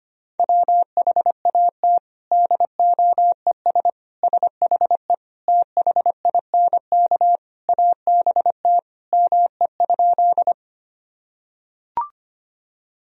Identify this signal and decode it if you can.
Morse code, heard as a keyed tone, about 25 wpm, 700 Hz. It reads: WHAT DOES SHE THINK ABT ME?
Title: Sentences from Top 200 Words and Abbreviations 25wpm